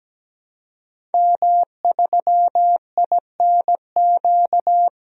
Morse code, heard as a keyed tone, about 17 wpm, 700 Hz. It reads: M3INQ
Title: DXCC-100 Call Signs - 5 Characters - 17wpm